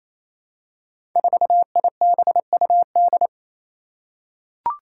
Morse code, heard as a keyed tone, about 28 wpm, 700 Hz. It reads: 4IBUD
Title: Five Challenging Letter-Number Combinations 28wpm